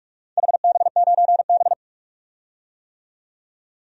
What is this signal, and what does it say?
Morse code, heard as a keyed tone, about 46 wpm, 700 Hz. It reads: HB9B
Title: DXCC-100 Call Signs - 4 Characters - 45wpm